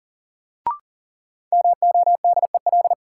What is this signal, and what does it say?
Morse code, heard as a keyed tone, about 40 wpm, 700 Hz. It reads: MODEL
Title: Sentences from Top 1000 Words - New Word Drill - 40wpm